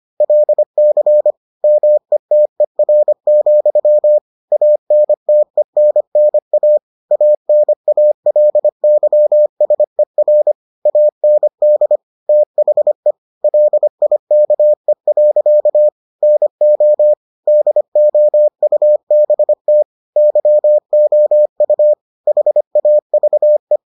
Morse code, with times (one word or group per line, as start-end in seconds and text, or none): 0.2-1.3 LC
1.6-4.2 METER,
4.5-6.8 ANTENNA
7.1-10.5 ANALYSER
10.9-12.0 AND
12.3-13.1 THE
13.4-15.9 LIKE.
16.2-17.1 NO
17.5-19.8 DOUBT
20.2-21.9 YOU
22.3-23.8 HAVE